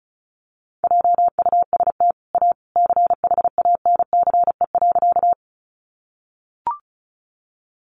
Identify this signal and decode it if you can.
Morse code, heard as a keyed tone, about 34 wpm, 700 Hz. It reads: JUST A CHANCE.
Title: Sentences from Top 800 Words 35wpm